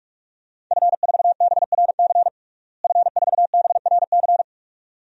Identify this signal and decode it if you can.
Morse code, heard as a keyed tone, about 46 wpm, 700 Hz. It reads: F4BRC F4BRC
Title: DXCC-100 Call Signs - 5 Characters - Repeated 2x - 45wpm